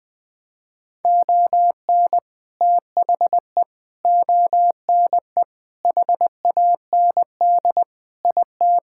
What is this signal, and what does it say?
Morse code, heard as a keyed tone, about 20 wpm, 700 Hz. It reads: ON THE ONE HAND IT
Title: Sentences with 4-Word Phrases - 20wpm